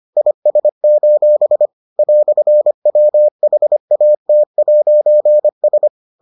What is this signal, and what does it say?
Morse code, heard as a keyed tone, about 25 wpm, 600 Hz. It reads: IS: "WHAT'S